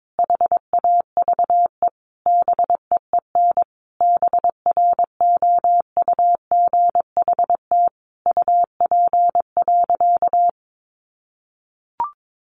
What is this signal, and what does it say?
Morse code, heard as a keyed tone, about 22 wpm, 700 Hz. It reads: HAVE BEEN BROUGHT UP.
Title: Sentences from Top 1500 Words 22wpm